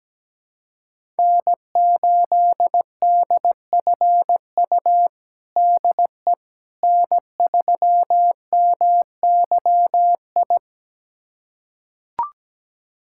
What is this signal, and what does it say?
Morse code, heard as a keyed tone, about 17 wpm, 700 Hz. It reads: N8DFU DE N3MYI
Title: USA Callsign DE Callsign - 5 Characters - 17wpm